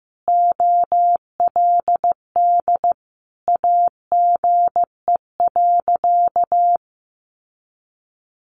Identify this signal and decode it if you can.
Morse code, heard as a keyed tone, about 14 wpm, 700 Hz. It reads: OLD AGE.
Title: Sentences from Top 1100 Words 15wpm